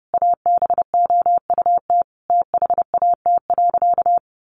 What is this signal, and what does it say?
Morse code, heard as a keyed tone, about 30 wpm, 700 Hz. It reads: ABOUT THAT.
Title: Sentences from Top 1500 Words 30wpm